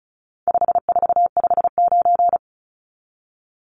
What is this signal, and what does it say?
Morse code, heard as a keyed tone, about 34 wpm, 700 Hz. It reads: 5459
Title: Numbers - 4-Digit 35wpm